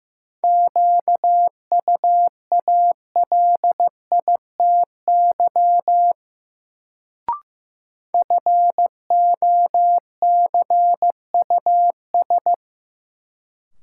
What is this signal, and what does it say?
Morse code, heard as a keyed tone, about 16 wpm, 700 Hz. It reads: QUALITY FOCUS
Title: Sentences from Top 1000 Words - Review Part 4 - New Word Drill - 15wpm